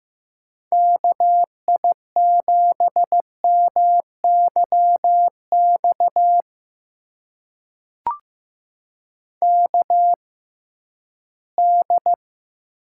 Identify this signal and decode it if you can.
Morse code, heard as a keyed tone, about 15 wpm, spaced as for 11 wpm, 700 Hz. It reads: KI7MYX KD
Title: USA Call Signs - 6 Characters - 6x Character Spacing - 15wpm